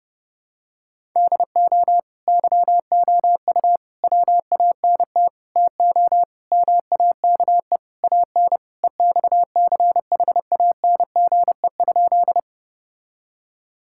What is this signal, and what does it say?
Morse code, heard as a keyed tone, about 30 wpm, 700 Hz. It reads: DO YOU WANT TO MAKE AN EXCHANGE?
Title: Sentences from Top 1000 Words - Review Part 6 - 30wpm